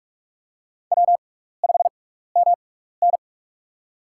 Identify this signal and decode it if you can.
Morse code, heard as a keyed tone, about 46 wpm, spaced as for 16 wpm, 700 Hz. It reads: W5MN
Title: USA Call Signs - 4 Characters - 6x Character Spacing - 45wpm